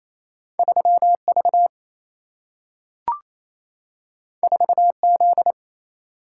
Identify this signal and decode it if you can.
Morse code, heard as a keyed tone, about 28 wpm, 700 Hz. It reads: 3 V 4 Z
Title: Single-Letter Single-Number 28wpm